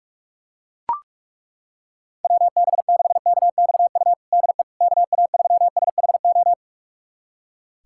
Morse code, heard as a keyed tone, about 41 wpm, 700 Hz. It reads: WB6KXU DE KA3SHO
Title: USA Callsign DE Callsign 45wpm